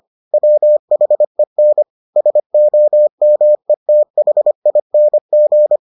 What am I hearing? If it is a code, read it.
Morse code, heard as a keyed tone, about 25 wpm, 600 Hz. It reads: WHEN SOMETHING